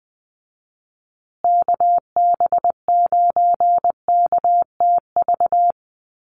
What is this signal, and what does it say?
Morse code, heard as a keyed tone, about 20 wpm, 700 Hz. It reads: KB9KTV
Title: USA Call Signs Repeated 2x 20wpm